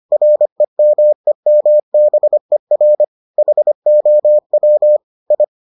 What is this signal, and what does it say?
Morse code, heard as a keyed tone, about 25 wpm, 600 Hz. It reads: REMEMBER HOW I